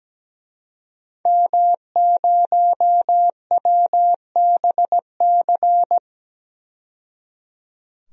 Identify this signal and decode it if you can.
Morse code, heard as a keyed tone, about 17 wpm, 700 Hz. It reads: M0WBC
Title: DXCC-100 Call Signs - 5 Characters - 17wpm